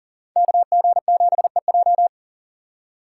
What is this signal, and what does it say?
Morse code, heard as a keyed tone, about 51 wpm, 700 Hz. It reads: KG7EJ